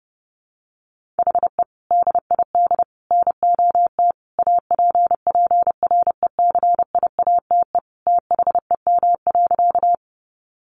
Morse code, heard as a keyed tone, about 30 wpm, 700 Hz. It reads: HE DID NOT APPRECIATE THEM.
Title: Sentences from Top 1300 Words 30wpm